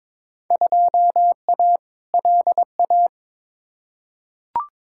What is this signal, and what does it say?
Morse code, heard as a keyed tone, about 22 wpm, 700 Hz. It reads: 2A LA
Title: Field Day Exchanges - 22wpm